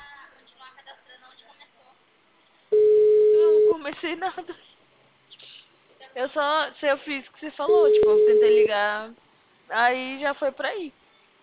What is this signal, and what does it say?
Ringback tone (Europe, 425 Hz), with speech.